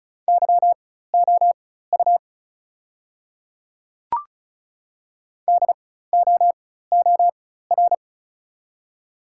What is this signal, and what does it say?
Morse code, heard as a keyed tone, about 34 wpm, spaced as for 16 wpm, 690 Hz. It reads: YOU DOOR
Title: 4x Character Spacing - Top 300 Words - 35wpm